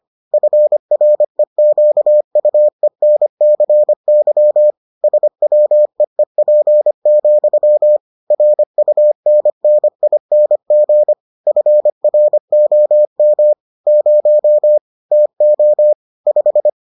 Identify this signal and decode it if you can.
Morse code, heard as a keyed tone, about 25 wpm, 600 Hz. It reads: FREQUENCY SWEEP, RUNNING FROM 0 TO 5